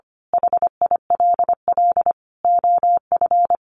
Morse code, heard as a keyed tone, about 25 wpm, 700 Hz. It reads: HILL OF
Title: A Sound Of Thunder 25wpm